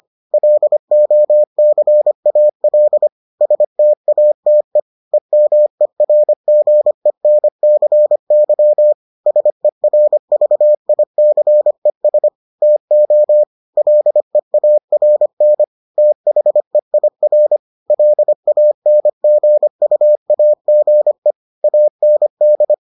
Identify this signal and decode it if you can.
Morse code, heard as a keyed tone, about 25 wpm, 600 Hz. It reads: LOCAL STATE EMERGENCY SERVICES TO LEARN THEIR LANGUAGE AND